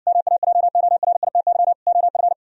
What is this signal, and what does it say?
Morse code, heard as a keyed tone, about 60 wpm, 700 Hz. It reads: MAJORITY OF